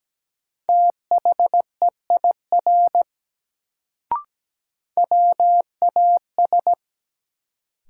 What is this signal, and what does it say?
Morse code, heard as a keyed tone, about 17 wpm, 700 Hz. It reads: THEIR WAS
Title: Sets of 2 Words - New Word Drill - 17wpm